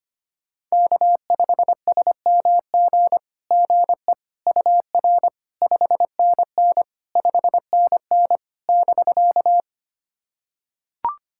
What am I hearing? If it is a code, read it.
Morse code, heard as a keyed tone, about 25 wpm, 700 Hz. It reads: K5SMG GE UR 5NN 5NN <BK>
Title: SOTA QSO Practice 25wpm